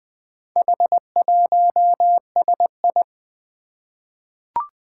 Morse code, heard as a keyed tone, about 20 wpm, 700 Hz. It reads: H1SI
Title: Four Challenging Letter-Number Combinations 20wpm